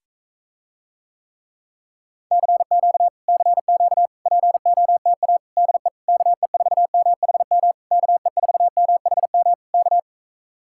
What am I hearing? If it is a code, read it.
Morse code, heard as a keyed tone, about 45 wpm, 700 Hz. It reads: CQ CQ POTA DE KE4MHM KE4MHM K